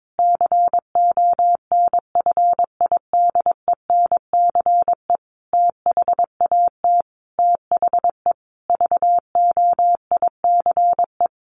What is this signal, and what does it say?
Morse code, heard as a keyed tone, about 22 wpm, 700 Hz. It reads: CONFIDENCE THAT THE VOICE
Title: Sentences from Top 1500 Words 22wpm